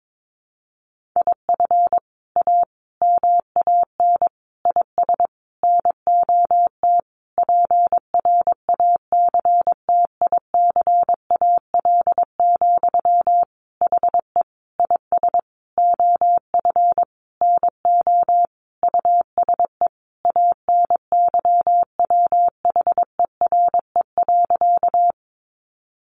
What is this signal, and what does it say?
Morse code, heard as a keyed tone, about 22 wpm, 700 Hz. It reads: IF A MAN IS NOT PRACTICAL, HE IS OF NO USE ANYWHERE.
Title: Sentences from Top 1400 Words 22wpm